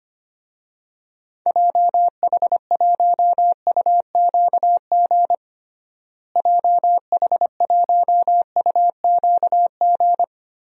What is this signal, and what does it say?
Morse code, heard as a keyed tone, about 25 wpm, 700 Hz. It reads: JH1UQG JH1UQG
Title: DXCC-100 Call Signs - 6 Characters - Repeated 3x - 25wpm